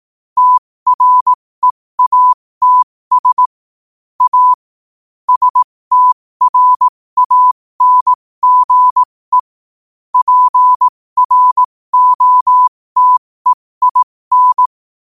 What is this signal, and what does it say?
Morse code, heard as a keyed tone, about 18 wpm, spaced as for 15 wpm, 1000 Hz. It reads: TREATS A STRANGE PROTEIN
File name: qotdfiles / 15wpm